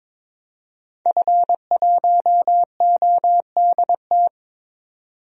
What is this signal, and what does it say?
Morse code, heard as a keyed tone, about 21 wpm, 700 Hz. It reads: F1ODT
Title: DXCC-100 Call Signs - 5 Characters - 22wpm